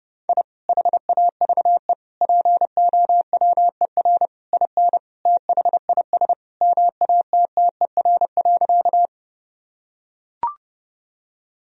Morse code, heard as a keyed tone, about 32 wpm, 700 Hz. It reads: I HAVE POWER IN THIS MATTER.
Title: Sentences from Top 1000 Words - Review Part 2 - 30wpm